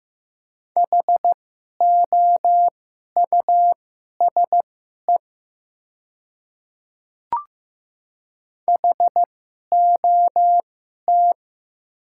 Morse code, heard as a keyed tone, about 16 wpm, spaced as for 11 wpm, 700 Hz. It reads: HOUSE HOT
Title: 2x Character Spacing - Top 300 Words - 15wpm